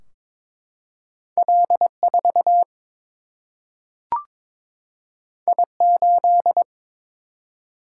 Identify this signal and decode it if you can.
Morse code, heard as a keyed tone, about 22 wpm, 700 Hz. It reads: L 4 I 8
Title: Single-Letter Single-Number 22wpm